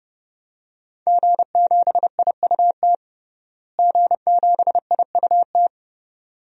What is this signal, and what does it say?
Morse code, heard as a keyed tone, about 29 wpm, 700 Hz. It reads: G7IUT G7IUT
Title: DXCC-100 Call Signs - 5 Characters - Repeated 2x - 30wpm